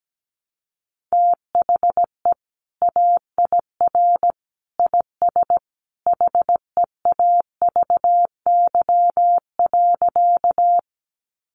Morse code, heard as a keyed tone, about 17 wpm, 700 Hz. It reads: THE AIR IS HEAVY.
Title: Sentences from Top 1100 Words 17wpm